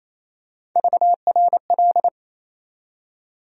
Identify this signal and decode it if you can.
Morse code, heard as a keyed tone, about 28 wpm, 700 Hz. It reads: VRL
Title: USA Call Sign Postfixes - 3 Letters 28wpm